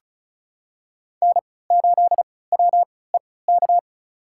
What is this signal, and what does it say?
Morse code, heard as a keyed tone, about 36 wpm, spaced as for 20 wpm, 700 Hz. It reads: N8WEK
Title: USA Call Signs - 3x Character Spacing - 35wpm